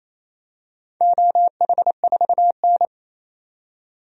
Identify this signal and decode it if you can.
Morse code, heard as a keyed tone, about 28 wpm, 700 Hz. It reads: OH4N
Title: DXCC-100 Call Signs - 4 Characters - 28wpm